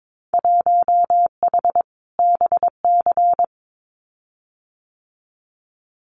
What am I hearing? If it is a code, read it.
Morse code, heard as a keyed tone, about 22 wpm, 700 Hz. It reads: 1H BC
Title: Winter Field Day Exchanges Encore 22wpm